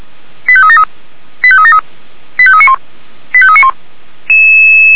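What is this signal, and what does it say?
Five-tone selective call (ZVEI), heard as quick runs of five tones, tones of about 70 ms. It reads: 85373 85373 85391 85391
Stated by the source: ZVEI five-tone selective call received by SDR, channel 464.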